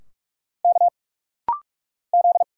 Morse code, heard as a keyed tone, about 46 wpm, 700 Hz. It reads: K Z
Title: Single Letters - Rapid-Fire 45wpm